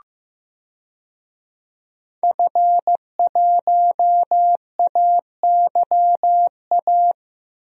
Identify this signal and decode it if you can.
Morse code, heard as a keyed tone, about 15 wpm, 700 Hz. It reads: F1AYA
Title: DXCC-100 Call Signs - 5 Characters - Repeated 2x - 15wpm